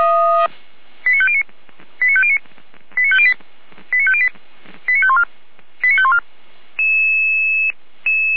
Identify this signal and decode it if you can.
Five-tone selective call (ZVEI), heard as quick runs of five tones, tones of about 70 ms. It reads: 89509 89509 89508 89508 89514 89514